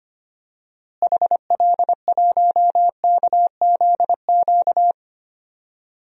Morse code, heard as a keyed tone, about 25 wpm, 700 Hz. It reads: HL1KZQ
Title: DXCC-100 Call Signs - 6 Characters - Repeated 2x - 25wpm